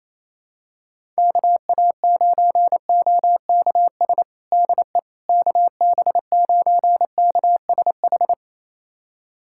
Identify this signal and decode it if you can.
Morse code, heard as a keyed tone, about 27 wpm, 700 Hz. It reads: KA9OKS DE KB9KSH
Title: USA Callsign DE Callsign - 6 Characters - 28wpm